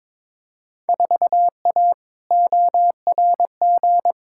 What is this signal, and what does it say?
Morse code, heard as a keyed tone, about 22 wpm, 700 Hz. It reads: 4A ORG
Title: Field Day Exchanges - 22wpm